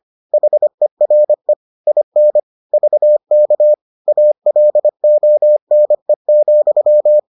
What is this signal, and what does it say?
Morse code, heard as a keyed tone, about 25 wpm, 600 Hz. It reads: HERE IN VK ALONE,